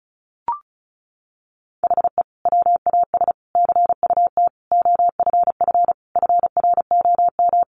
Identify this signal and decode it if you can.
Morse code, heard as a keyed tone, about 34 wpm, 700 Hz. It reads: HE WAS CUT OFF FROM
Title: Sentences from Top 1100 Words 35wpm